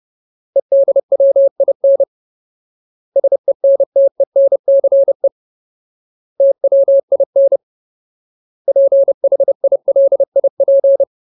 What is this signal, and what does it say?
Morse code, heard as a keyed tone, about 30 wpm, 550 Hz. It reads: EDWIN SENTENCE TWIN PHILIP
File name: Ultimate Word List / 30WPM Ultimate